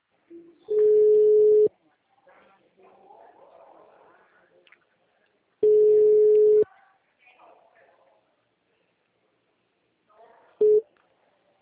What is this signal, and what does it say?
Ringback tone (Europe, 425 Hz), with speech.